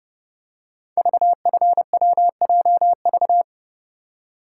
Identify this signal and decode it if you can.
Morse code, heard as a keyed tone, about 30 wpm, 700 Hz. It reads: VFWJV
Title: Code Groups - 5 Letters - 30wpm